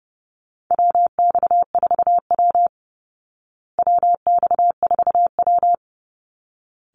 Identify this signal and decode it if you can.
Morse code, heard as a keyed tone, about 30 wpm, 700 Hz. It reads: WX4W WX4W
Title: Call Signs Repeated 3x - 30wpm